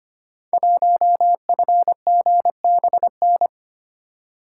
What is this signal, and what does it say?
Morse code, heard as a keyed tone, about 25 wpm, 700 Hz. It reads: 1FGBN
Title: Code Groups - 5 Letter:Numbers - 25wpm